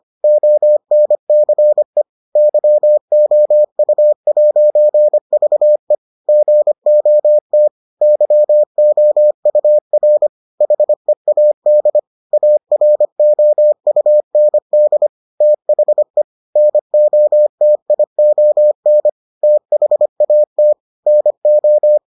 Morse code, heard as a keyed tone, about 25 wpm, 600 Hz. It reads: ONCE YOU'VE GOT YOUR HEAD AROUND THE NOTION THAT NO